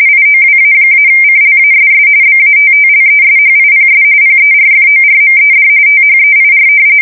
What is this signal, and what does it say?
RTTY, about 45.45 baud. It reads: IOWA-CLASS BATTLESHIPS PERFORMED THEIR
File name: fsk170 / battleshipnj